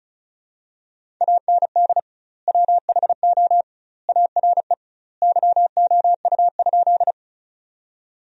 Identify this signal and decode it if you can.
Morse code, heard as a keyed tone, about 34 wpm, 700 Hz. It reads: AND WHO ARE YOU?
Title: 2x Word-Spacing - Sentences from Top 100 Words - 35wpm